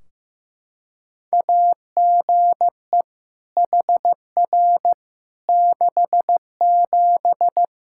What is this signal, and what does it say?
Morse code, heard as a keyed tone, about 15 wpm, 700 Hz. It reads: AGE HR 67
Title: Practice QSO - Exchange 3 - More Info - 15wpm